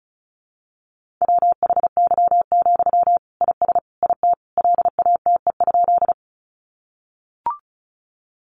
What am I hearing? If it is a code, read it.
Morse code, heard as a keyed tone, about 34 wpm, 700 Hz. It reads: WHY, IS IT LATE?
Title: Sentences from Top 300 Words 35wpm